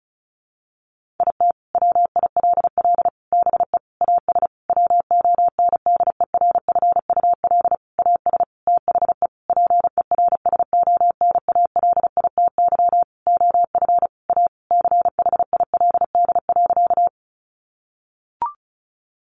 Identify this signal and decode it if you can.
Morse code, heard as a keyed tone, about 34 wpm, 700 Hz. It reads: IT WILL BE AS WONDERFUL AS THE PERSONALITY OF A CHILD.
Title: Sentences from Top 1400 Words 35wpm